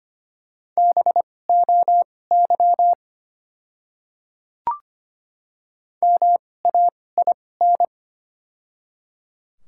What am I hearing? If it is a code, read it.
Morse code, heard as a keyed tone, about 25 wpm, spaced as for 18 wpm, 690 Hz. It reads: BOY MAIN
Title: 2x Character Spacing - Top 300 Words - 25wpm